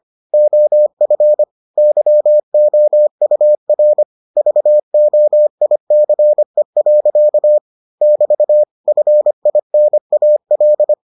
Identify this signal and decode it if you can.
Morse code, heard as a keyed tone, about 25 wpm, 600 Hz. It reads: OF YOUR VOICE. <BT> FINAL